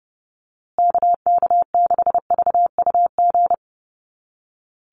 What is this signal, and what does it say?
Morse code, heard as a keyed tone, about 30 wpm, 700 Hz. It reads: KK6VUG
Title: USA Call Signs Repeated 3x 30wpm